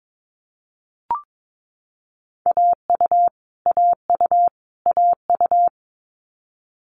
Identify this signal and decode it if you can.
Morse code, heard as a keyed tone, about 22 wpm, 700 Hz. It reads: AU AU AU
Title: Multi-letter Phonemes Repeated 3x 22wpm